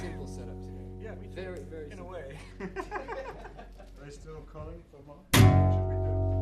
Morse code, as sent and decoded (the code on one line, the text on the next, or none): none